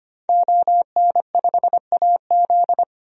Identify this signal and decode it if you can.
Morse code, heard as a keyed tone, about 25 wpm, 700 Hz. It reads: ON5AZ
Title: DXCC-100 Call Signs - 5 Characters - 25wpm